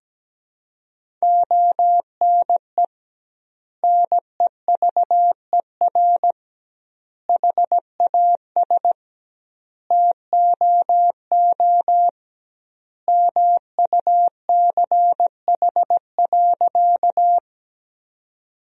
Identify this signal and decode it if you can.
Morse code, heard as a keyed tone, about 17 wpm, 700 Hz. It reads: ONE NEVER HAS TOO MUCH.
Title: 2x Word-Spacing - Sentences from Top 300 Words - 17wpm